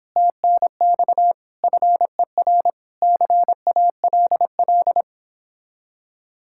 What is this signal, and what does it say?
Morse code, heard as a keyed tone, about 26 wpm, 700 Hz. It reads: TNX FER CALL
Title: Speed-Racing Common QSO Phrases Encore - 22wpm